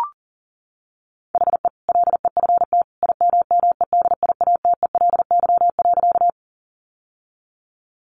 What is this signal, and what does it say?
Morse code, heard as a keyed tone, about 40 wpm, 700 Hz. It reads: HE LEFT IMMEDIATELY.
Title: Sentences from Top 1100 Words 40wpm